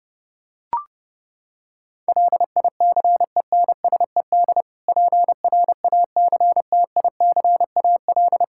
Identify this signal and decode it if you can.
Morse code, heard as a keyed tone, about 29 wpm, 700 Hz. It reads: LICENSED PRACTICAL
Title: Practice QSO - Exchange 3 - More Info - 30wpm